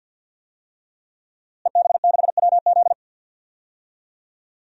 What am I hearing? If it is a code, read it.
Morse code, heard as a keyed tone, about 51 wpm, 700 Hz. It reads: EB6WZ
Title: DXCC-100 Call Signs - 5 Characters - Repeated 3x - 50wpm